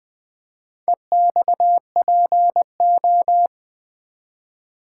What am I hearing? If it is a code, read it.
Morse code, heard as a keyed tone, about 20 wpm, 700 Hz. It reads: EXPO
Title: Build A Word - 2 Letters At A Time - 6-Letter Words - 20wpm